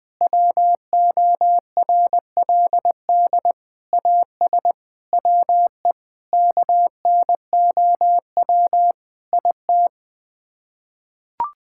Morse code, heard as a keyed tone, about 20 wpm, 700 Hz. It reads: WORLD AS WE KNOW IT
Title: Sets of 8 Words - 20wpm